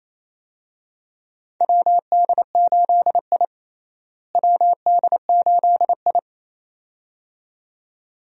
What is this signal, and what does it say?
Morse code, heard as a keyed tone, about 28 wpm, 700 Hz. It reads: WD8I WD8I
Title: USA Call Signs - 4 Characters - Repeated 2x - 28wpm